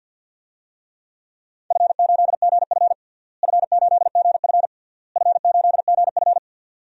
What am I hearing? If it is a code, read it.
Morse code, heard as a keyed tone, about 52 wpm, 700 Hz. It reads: F8GF F8GF F8GF
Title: DXCC-100 Call Signs - 4 Characters - Repeated 3x - 50wpm